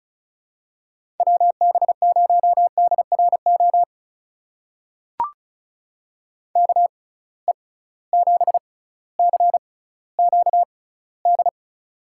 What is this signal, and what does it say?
Morse code, heard as a keyed tone, about 36 wpm, spaced as for 18 wpm, 700 Hz. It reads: WB0DRO KE7CQD
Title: USA Call Signs - 6 Characters - 6x Character Spacing - 35wpm